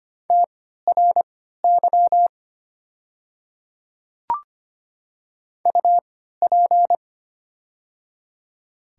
Morse code, heard as a keyed tone, about 25 wpm, spaced as for 14 wpm, 690 Hz. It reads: TRY UP
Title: 3x Character Spacing - Top 300 Words - 25wpm